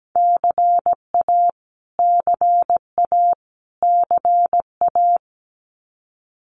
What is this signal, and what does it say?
Morse code, heard as a keyed tone, about 17 wpm, 700 Hz. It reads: CA CA CA
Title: US States Repeated 3x 17wpm